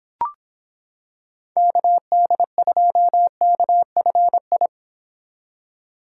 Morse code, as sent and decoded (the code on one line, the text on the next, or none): -.- -.. ..--- -.- ..-. ..
KD2KFI